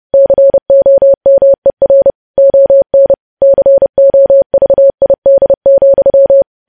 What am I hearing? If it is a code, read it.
Morse code, heard as a keyed tone, about 30 wpm, 550 Hz. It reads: COMER ON COVID,
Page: news headlines in morse code at 30 wpm